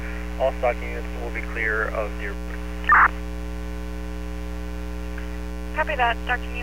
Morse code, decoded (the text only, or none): none